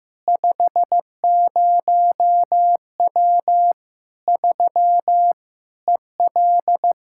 Morse code, heard as a keyed tone, about 15 wpm, 700 Hz. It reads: 50W 3 EL
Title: Practice QSO - Exchange 2 - Info - 15wpm